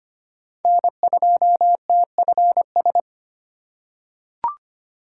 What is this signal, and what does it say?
Morse code, heard as a keyed tone, about 25 wpm, 700 Hz. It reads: N2TFS
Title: USA Call Signs - 5 Characters - 2x Character Spacing - 25wpm